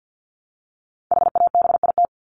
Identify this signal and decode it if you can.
Morse code, heard as a keyed tone, about 50 wpm, 700 Hz. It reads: HABIT
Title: Top 1751-2000 Words 50wpm